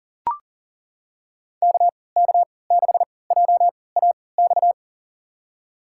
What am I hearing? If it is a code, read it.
Morse code, heard as a keyed tone, about 40 wpm, spaced as for 23 wpm, 700 Hz. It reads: KK6JAX
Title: USA Call Signs - 6 Characters - 3x Character Spacing - 40wpm